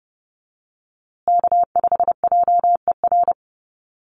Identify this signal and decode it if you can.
Morse code, heard as a keyed tone, about 30 wpm, 700 Hz. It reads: K5JER
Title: USA Call Signs Repeated 3x 30wpm